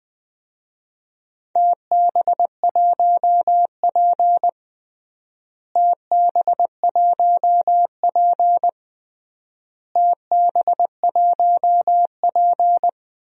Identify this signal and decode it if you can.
Morse code, heard as a keyed tone, about 21 wpm, 700 Hz. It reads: TB1P TB1P TB1P
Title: DXCC-100 Call Signs - 4 Characters - Repeated 3x - 20wpm